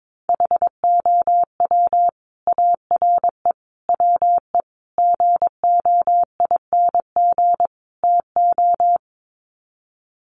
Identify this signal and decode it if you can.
Morse code, heard as a keyed tone, about 22 wpm, 700 Hz. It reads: HOW ARE WE GOING TO
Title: Sets of 5 Words Encore 22wpm